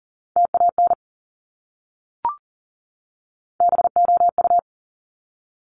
Morse code, heard as a keyed tone, about 40 wpm, 700 Hz. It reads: TAN BOU
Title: Top Three Letter Combinations in Words 40wpm